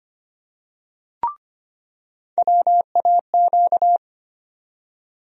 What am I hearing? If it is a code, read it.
Morse code, heard as a keyed tone, about 25 wpm, 700 Hz. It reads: WAQ
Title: USA Call Sign Postfixes - 3 Letters 25wpm